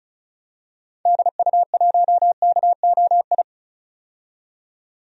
Morse code, heard as a keyed tone, about 36 wpm, 700 Hz. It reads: DU1KOI
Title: DXCC-100 Call Signs - 6 Characters - 35wpm